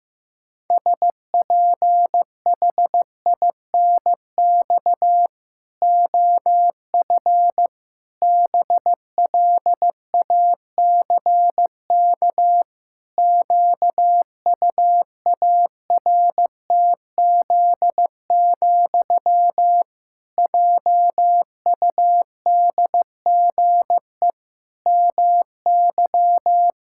Morse code, heard as a keyed tone, about 15 wpm, 700 Hz. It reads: SPHINX OF BLACK QUARTZ, JUDGE MY
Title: Pangram Challenge v2 - 15wpm